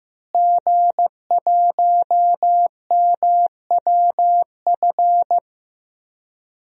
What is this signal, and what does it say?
Morse code, heard as a keyed tone, about 15 wpm, 700 Hz. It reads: G1MWF
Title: DXCC-100 Call Signs - 5 Characters - Repeated 3x - 15wpm